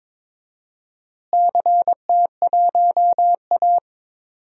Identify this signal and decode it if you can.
Morse code, heard as a keyed tone, about 22 wpm, 700 Hz. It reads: CT1A
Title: DXCC-100 Call Signs - 4 Characters - Repeated 3x - 22wpm